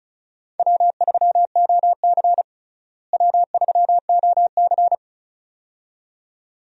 Morse code, heard as a keyed tone, about 36 wpm, 700 Hz. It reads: W3OC W3OC
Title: USA Call Signs - 4 Characters - Repeated 2x - 35wpm